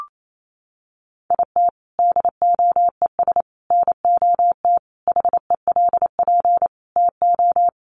Morse code, heard as a keyed tone, about 28 wpm, 700 Hz. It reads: IT DOES NOT HELP TO
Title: Sentences from Top 1500 Words 28wpm